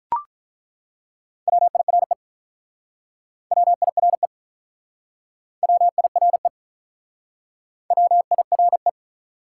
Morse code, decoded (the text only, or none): WIRE WIRE WIRE WIRE